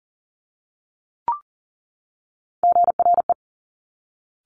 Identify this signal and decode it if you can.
Morse code, heard as a keyed tone, about 42 wpm, 700 Hz. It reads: GRE
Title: Top Three Letter Combinations in Words 40wpm